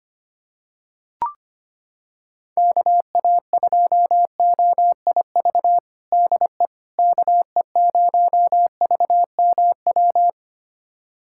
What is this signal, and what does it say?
Morse code, heard as a keyed tone, about 25 wpm, 700 Hz. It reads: KA2OIV DE KE0VMW
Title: USA Callsign DE Callsign - 6 Characters - 25wpm